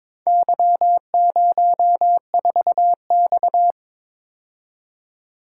Morse code, heard as a keyed tone, about 22 wpm, 690 Hz. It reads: Y04X
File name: practice / 22wpm / Four Letter-Number Combinations - 22wpm